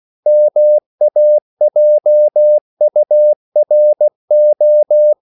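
Morse code, heard as a keyed tone, about 16 wpm, 600 Hz. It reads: MAJURO
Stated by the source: CW practice of national capitals at 16 words per minute, from ON6ZQ.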